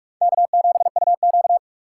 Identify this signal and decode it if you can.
Morse code, heard as a keyed tone, about 42 wpm, 700 Hz. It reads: K7UQ